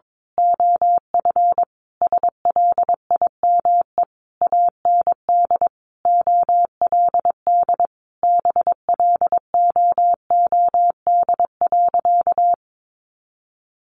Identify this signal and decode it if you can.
Morse code, heard as a keyed tone, about 22 wpm, 700 Hz. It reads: OF SLIME AND OLD BLOOD.
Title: A Sound Of Thunder 22wpm